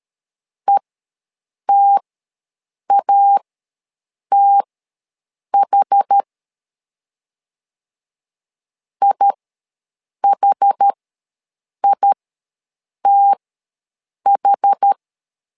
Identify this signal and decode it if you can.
Morse code, heard as a keyed tone, about 13 wpm, spaced as for 7 wpm, 800 Hz. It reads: ETATH IHITH
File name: LEZIONI / 2 UHDN / 2 UH ws 12wpm 1730